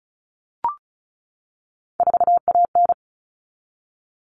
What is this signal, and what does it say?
Morse code, heard as a keyed tone, about 34 wpm, 700 Hz. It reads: 4AN
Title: All Cut Numbers - 3-Digit 35wpm